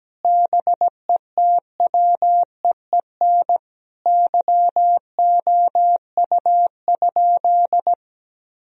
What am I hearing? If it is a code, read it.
Morse code, heard as a keyed tone, about 17 wpm, 700 Hz. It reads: BETWEEN YOU?
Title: Sentences from Top 1000 Words - Review Part 1 - 17wpm